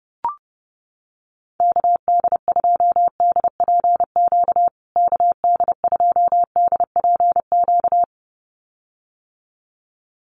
Morse code, heard as a keyed tone, about 30 wpm, 700 Hz. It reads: KD2DPQ KD2DPQ
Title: USA Call Signs Repeated 2x 30wpm